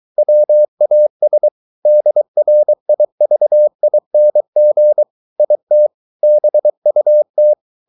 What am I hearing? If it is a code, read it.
Morse code, heard as a keyed tone, about 23 wpm, 600 Hz. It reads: WAS DRIVING IT BUT
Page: Romanen Never af Ken Follet på engelsk.